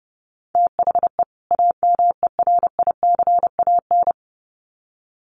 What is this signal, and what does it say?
Morse code, heard as a keyed tone, about 28 wpm, 700 Hz. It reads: THE AMERICAN
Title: Sets of 2 Words 30wpm